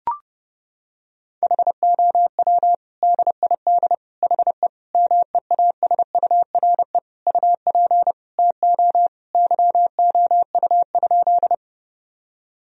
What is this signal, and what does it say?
Morse code, heard as a keyed tone, about 30 wpm, 700 Hz. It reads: HOW DID HE MEASURE UP TO YOU?
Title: Sentences from Top 1000 Words - Review Part 2 - 30wpm